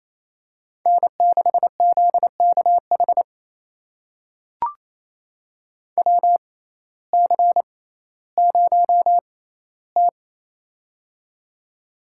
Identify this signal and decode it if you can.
Morse code, heard as a keyed tone, about 27 wpm, spaced as for 15 wpm, 700 Hz. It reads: N6ZKH WC0T
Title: USA Call Signs - 6x Character Spacing - 28wpm